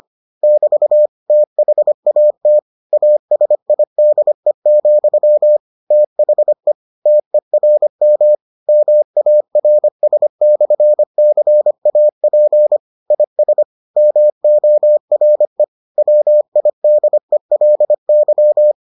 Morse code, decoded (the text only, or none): <BT> THAT ASIDE, THE TERM MARS/CAP IS MORE WIDELY